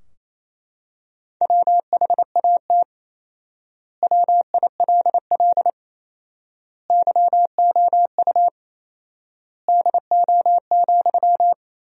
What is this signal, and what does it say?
Morse code, heard as a keyed tone, about 28 wpm, 700 Hz. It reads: WHAT WILL YOU DO,
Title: 4x Word-Spacing - Sentences from Top 100 Words - 28wpm